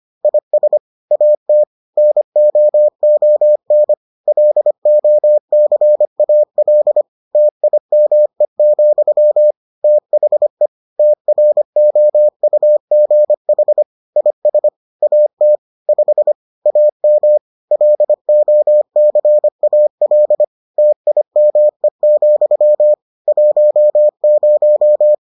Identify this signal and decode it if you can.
Morse code, heard as a keyed tone, about 25 wpm, 600 Hz. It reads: IS AT NOON LOCAL TIME, THE TROUGH IS AT 5 AM LOCAL TIME, 10